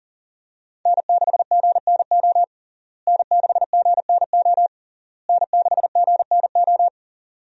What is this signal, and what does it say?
Morse code, heard as a keyed tone, about 40 wpm, 700 Hz. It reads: N6GNO N6GNO N6GNO
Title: USA Call Signs - 5 Characters - Repeated 3x - 40wpm